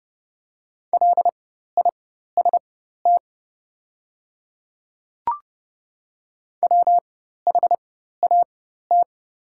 Morse code, heard as a keyed tone, about 30 wpm, spaced as for 14 wpm, 700 Hz. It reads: LIST WHAT
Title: 4x Character Spacing - Top 500 Words - 30wpm